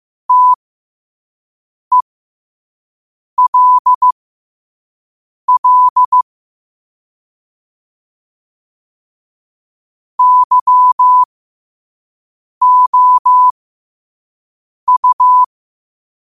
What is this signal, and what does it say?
Morse code, heard as a keyed tone, about 15 wpm, spaced as for 5 wpm, 1000 Hz. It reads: TELL YOU
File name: qotdfiles / 5wpm